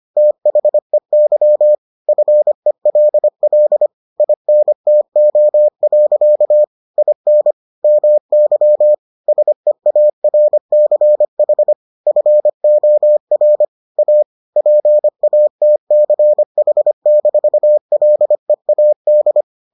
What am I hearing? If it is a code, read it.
Morse code, heard as a keyed tone, about 25 wpm, 600 Hz. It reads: THEY FELL INTO. IN MY SEARCH FOR A PATCH-LEAD